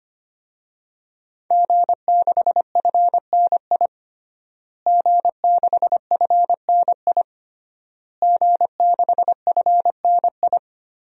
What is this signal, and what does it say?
Morse code, heard as a keyed tone, about 25 wpm, 700 Hz. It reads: G6FNI G6FNI G6FNI
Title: DXCC-100 Call Signs - 5 Characters - Repeated 3x - 25wpm